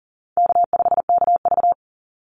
Morse code, heard as a keyed tone, about 42 wpm, 700 Hz. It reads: K5KV
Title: Easy Call Signs - 40wpm